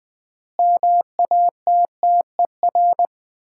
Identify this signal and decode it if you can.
Morse code, heard as a keyed tone, about 20 wpm, 700 Hz. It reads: MATTER
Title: 3x Word-Spacing - Sets of 4 Words - 20wpm